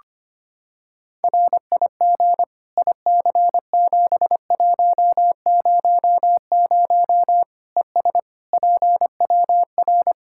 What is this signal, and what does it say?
Morse code, heard as a keyed tone, about 25 wpm, 700 Hz. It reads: RIG IC7100 ES PWR
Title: K1USN SST QSO Practice 25wpm